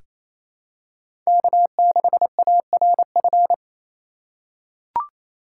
Morse code, heard as a keyed tone, about 28 wpm, 700 Hz. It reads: K6ARF
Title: USA Call Signs - 5 Characters - 2x Character Spacing - 28wpm